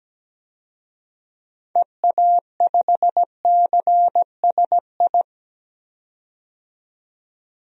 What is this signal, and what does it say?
Morse code, heard as a keyed tone, about 17 wpm, 700 Hz. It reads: EA5CSI
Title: DXCC-100 Call Signs - 6 Characters - 17wpm